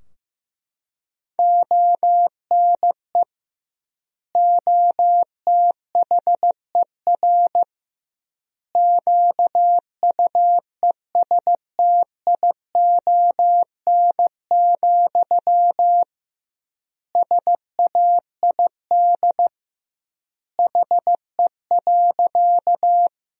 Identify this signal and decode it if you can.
Morse code, heard as a keyed tone, about 16 wpm, 700 Hz. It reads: ONE OTHER QUESTION, SAID HE.
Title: 2x Word-Spacing - Sentences from Top 400 Words - 15wpm